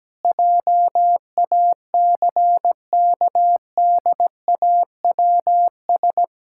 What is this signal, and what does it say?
Morse code, heard as a keyed tone, about 17 wpm, 700 Hz. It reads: JACKDAWS
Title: Pangram Challenge v2 - 17wpm